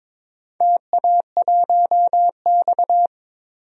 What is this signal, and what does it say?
Morse code, heard as a keyed tone, about 22 wpm, 700 Hz. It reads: TA1X
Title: DXCC-100 Call Signs - 4 Characters - Repeated 2x - 22wpm